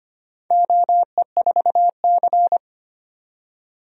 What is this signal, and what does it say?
Morse code, heard as a keyed tone, about 25 wpm, 700 Hz. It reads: OE4C
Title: DXCC-100 Call Signs - 4 Characters - 25wpm